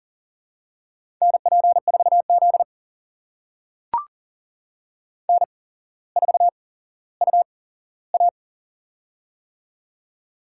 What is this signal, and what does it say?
Morse code, heard as a keyed tone, about 40 wpm, spaced as for 17 wpm, 700 Hz. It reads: NP4Z N4UA
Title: USA Call Signs - 4 Characters - 8x Character Spacing - 40wpm